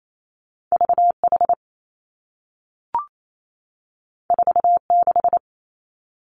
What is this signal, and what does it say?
Morse code, heard as a keyed tone, about 28 wpm, 700 Hz. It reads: V H 4 6
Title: Two Challenging Letter-Number Combinations 28wpm